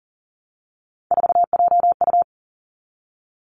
Morse code, heard as a keyed tone, about 37 wpm, 700 Hz. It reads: 4JU
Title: Three Challenging Letter-Number Combinations 40wpm